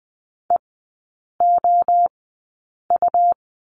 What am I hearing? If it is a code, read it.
Morse code, heard as a keyed tone, about 20 wpm, spaced as for 8 wpm, 690 Hz. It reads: EOU
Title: 159 Lesson 26e - Letter g - Three Characters 20wpm